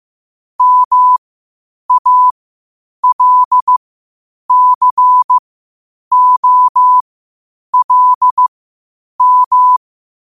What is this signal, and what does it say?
Morse code, heard as a keyed tone, about 15 wpm, spaced as for 8 wpm, 1000 Hz. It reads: MALCOLM
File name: qotdfiles / 8wpm